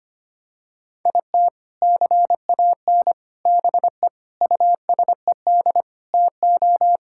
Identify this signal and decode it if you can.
Morse code, heard as a keyed tone, about 25 wpm, 700 Hz. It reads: IT CAN BE USED TO
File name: practice / 25wpm / Sentences with 4-Word Phrases - 25wpm